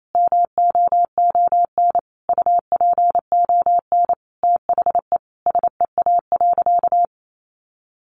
Morse code, heard as a keyed tone, about 28 wpm, 700 Hz. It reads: MOON UPON THE SEA.
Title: Sentences from Top 1500 Words 28wpm